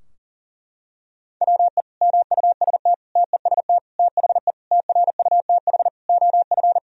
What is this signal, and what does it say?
Morse code, heard as a keyed tone, about 40 wpm, 700 Hz. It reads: WE MUST TEST THE TRUTH OF
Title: Sentences from Top 1000 Words - Review Part 3 - 40wpm